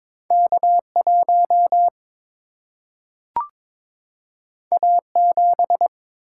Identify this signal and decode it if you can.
Morse code, heard as a keyed tone, about 21 wpm, 700 Hz. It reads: K 1 A 7
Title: Single-Letter Single-Number 22wpm